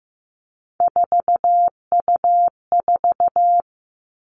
Morse code, heard as a keyed tone, about 15 wpm, 700 Hz. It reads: 4U4
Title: Three Challenging Letter-Number Combinations 15wpm